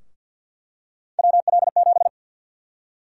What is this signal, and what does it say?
Morse code, heard as a keyed tone, about 49 wpm, 700 Hz. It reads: WL7
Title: USA Call Sign Prefixes - 50wpm